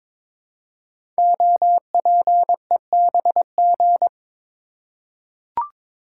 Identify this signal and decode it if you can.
Morse code, heard as a keyed tone, about 22 wpm, 700 Hz. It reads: OPEBG
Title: Code Groups - 5 Letter:Numbers - 22wpm